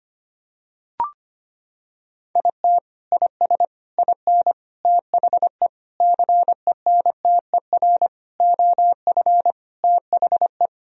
Morse code, heard as a keyed tone, about 25 wpm, 700 Hz. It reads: IT IS IN THE CENTER OF THE
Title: Sentences from Top 1000 Words - Review Part 2 - 25wpm